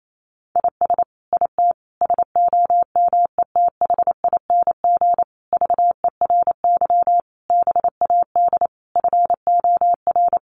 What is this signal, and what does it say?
Morse code, heard as a keyed tone, about 28 wpm, 700 Hz. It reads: IS IT SOMETHING VERY BAD FOR
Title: Sentences from Top 700 Words 28wpm